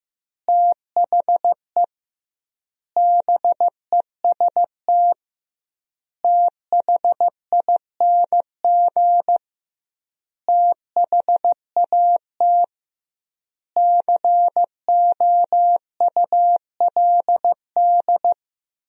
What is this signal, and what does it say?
Morse code, heard as a keyed tone, about 15 wpm, 700 Hz. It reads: THE BEST THING THAT COULD
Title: 2x Word-Spacing - Sentences from Top 400 Words - 15wpm